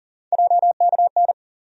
Morse code, heard as a keyed tone, about 40 wpm, 700 Hz. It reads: JKN
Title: USA Call Sign Postfixes - 3 Letters 40wpm